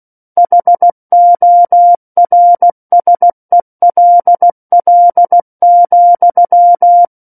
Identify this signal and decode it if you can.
Morse code, heard as a keyed tone, about 16 wpm, 700 Hz. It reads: HORSELL,
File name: Book One / 07-Chapter 7 - 16 WPM